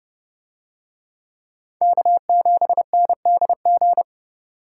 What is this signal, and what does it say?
Morse code, heard as a keyed tone, about 31 wpm, 700 Hz. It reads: K7NDG